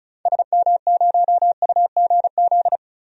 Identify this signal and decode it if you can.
Morse code, heard as a keyed tone, about 36 wpm, 700 Hz. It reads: SM0UGZ
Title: DXCC-100 Call Signs - 6 Characters - Repeated 3x - 35wpm